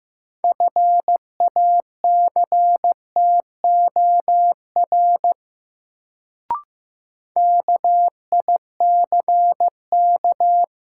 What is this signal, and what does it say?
Morse code, heard as a keyed tone, about 15 wpm, 700 Hz. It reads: FACTOR KICK
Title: Sentences from Top 1300 Words - New Word Drill - 15wpm